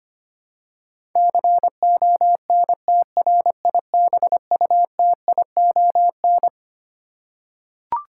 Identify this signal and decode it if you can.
Morse code, heard as a keyed tone, about 25 wpm, 700 Hz. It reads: CONTRIBUTION
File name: Twelve Letter Words - 25wpm